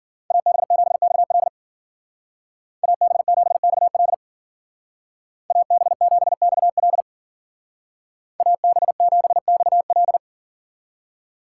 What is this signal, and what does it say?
Morse code, heard as a keyed tone, about 50 wpm, 700 Hz. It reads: AB7XL AB7XL AB7XL AB7XL